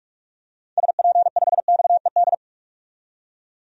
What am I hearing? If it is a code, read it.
Morse code, heard as a keyed tone, about 41 wpm, 700 Hz. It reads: SP5XED
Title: DXCC-100 Call Signs - 6 Characters - 45wpm